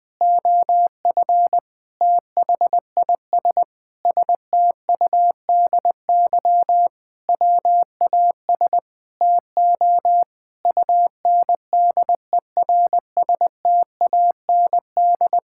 Morse code, heard as a keyed tone, about 20 wpm, 700 Hz. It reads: OF THIS STUDY WAS TO UNDERSTAND
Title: Sentences with 5-Word Phrases - 20wpm